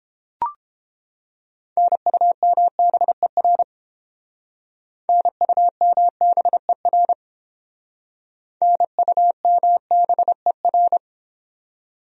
Morse code, heard as a keyed tone, about 31 wpm, 700 Hz. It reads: NUMBER NUMBER NUMBER